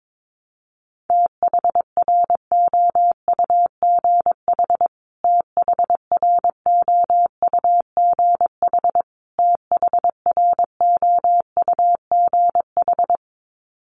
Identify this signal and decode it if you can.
Morse code, heard as a keyed tone, about 22 wpm, 700 Hz. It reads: THROUGH THROUGH THROUGH
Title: Top 500 Words Repeated 22wpm